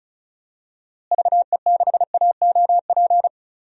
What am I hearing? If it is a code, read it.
Morse code, heard as a keyed tone, about 36 wpm, 700 Hz. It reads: VE6AOP
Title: DXCC-100 Call Signs - 6 Characters - Repeated 2x - 35wpm